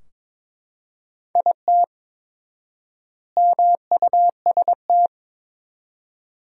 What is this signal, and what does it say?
Morse code, heard as a keyed tone, about 22 wpm, 700 Hz. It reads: IT MUST
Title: 4x Word-Spacing - Sentences from Top 200 Words - 22wpm